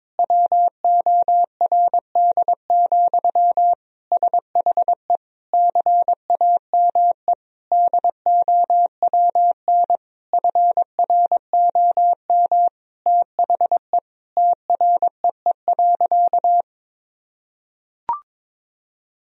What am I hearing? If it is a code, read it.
Morse code, heard as a keyed tone, about 21 wpm, 700 Hz. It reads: WORD, SHE CAME DOWN FROM THE TREE.
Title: Sentences from Top 1000 Words - Review Part 1 - 22wpm